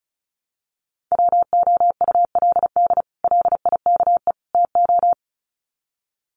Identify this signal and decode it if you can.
Morse code, heard as a keyed tone, about 34 wpm, 700 Hz. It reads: WOULD LIKE TO
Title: Sets of 3 Words 35wpm